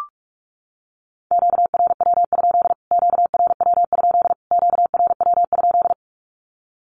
Morse code, heard as a keyed tone, about 44 wpm, 700 Hz. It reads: QRW? QRW? QRW?
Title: Q-Codes Repeated 3x 45wpm